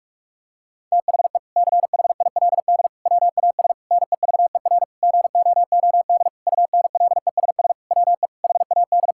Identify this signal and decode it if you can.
Morse code, heard as a keyed tone, about 46 wpm, 700 Hz. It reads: THE CHILD WAS NEVER GOOD UNLESS WE HAD
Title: Sentences from Top 1000 Words - Review Part 6 - 45wpm